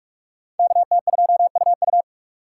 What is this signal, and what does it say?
Morse code, heard as a keyed tone, about 43 wpm, 700 Hz. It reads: KT2UU